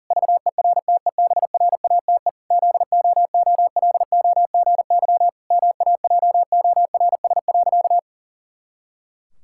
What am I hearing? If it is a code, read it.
Morse code, heard as a keyed tone, about 37 wpm, 700 Hz. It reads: VERTEBRATE ZOOLOGY MAJORS.
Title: Pangram Challenge v2 - 40wpm